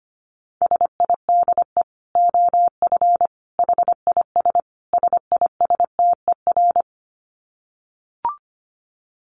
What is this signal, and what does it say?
Morse code, heard as a keyed tone, about 25 wpm, 700 Hz. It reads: SIDE OF HIS SISTER
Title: Dit Challenge 25wpm